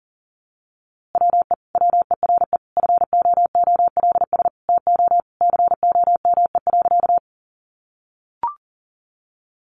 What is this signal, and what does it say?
Morse code, heard as a keyed tone, about 40 wpm, 700 Hz. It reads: WE WERE FOOLS TO COME.
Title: A Sound Of Thunder 40wpm